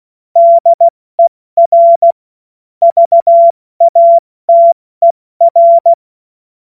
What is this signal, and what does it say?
Morse code, heard as a keyed tone, about 16 wpm, spaced as for 14 wpm, 680 Hz. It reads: DER VATER
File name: Morsekurs / Witze / Witz 24. August 2021, 70 F80